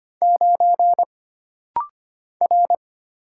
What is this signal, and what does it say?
Morse code, heard as a keyed tone, about 25 wpm, 700 Hz. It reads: 9 R
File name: Single Character (Letter:Number:Punctuation:Prosign) - Rapid-Fire - 25wpm